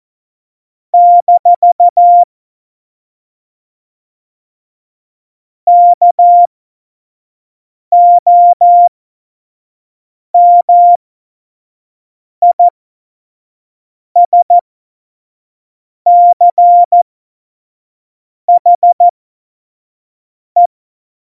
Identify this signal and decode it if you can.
Morse code, heard as a keyed tone, about 14 wpm, spaced as for 5 wpm, 700 Hz. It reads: - KOMISCHE
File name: Tempo 25 F70